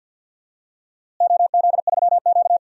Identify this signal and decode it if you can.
Morse code, heard as a keyed tone, about 52 wpm, 700 Hz. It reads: OZ3Q
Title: DXCC-100 Call Signs - 4 Characters - Repeated 3x - 50wpm